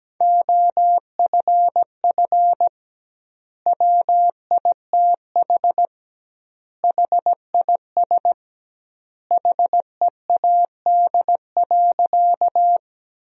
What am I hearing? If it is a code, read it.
Morse code, heard as a keyed tone, about 17 wpm, 700 Hz. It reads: OFF WITH HIS HEAD.
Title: 2x Word-Spacing - Sentences from Top 300 Words - 17wpm